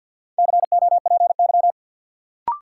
Morse code, heard as a keyed tone, about 53 wpm, 700 Hz. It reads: COPY
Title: Sentences from Top 1400 Words - New Word Drill Alt - 50wpm